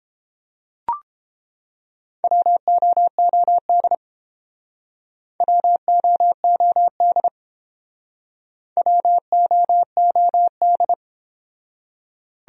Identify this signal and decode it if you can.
Morse code, heard as a keyed tone, about 28 wpm, 700 Hz. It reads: WOOD WOOD WOOD